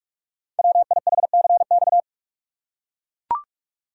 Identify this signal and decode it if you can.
Morse code, heard as a keyed tone, about 44 wpm, 700 Hz. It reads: WIHCX
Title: Code Groups - 5 Letters - 45wpm